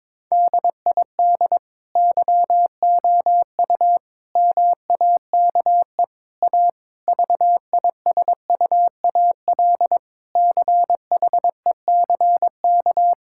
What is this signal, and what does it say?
Morse code, heard as a keyed tone, about 22 wpm, 700 Hz. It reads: DID YOU MAKE A VISUAL CHECK
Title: Sentences from Top 1000 Words - Review Part 6 - 22wpm